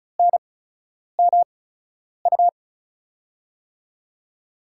Morse code, heard as a keyed tone, about 36 wpm, spaced as for 10 wpm, 700 Hz. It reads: NMU
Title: USA Call Signs - 5 Characters - 8x Character Spacing - 35wpm